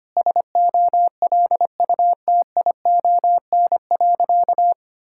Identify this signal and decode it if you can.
Morse code, heard as a keyed tone, about 25 wpm, 700 Hz. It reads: SOLUTION.
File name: practice / 25wpm / Sentences with 3-Word Phrases - 25wpm